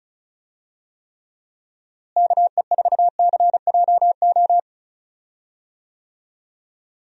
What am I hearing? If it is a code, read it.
Morse code, heard as a keyed tone, about 36 wpm, 700 Hz. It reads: KE4CJO
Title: USA Call Signs - 6 Characters - Repeated 2x - 35wpm